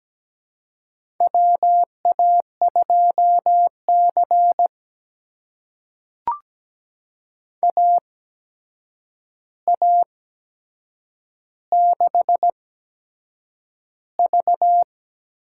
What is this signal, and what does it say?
Morse code, heard as a keyed tone, about 17 wpm, spaced as for 7 wpm, 700 Hz. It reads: WA2C AA6V
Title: USA Call Signs - 4 Characters - 8x Character Spacing - 17wpm